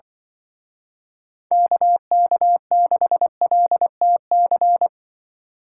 Morse code, heard as a keyed tone, about 24 wpm, 700 Hz. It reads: KK6LTC
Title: Speed Racing - USA 2x3 Call Signs 20wpm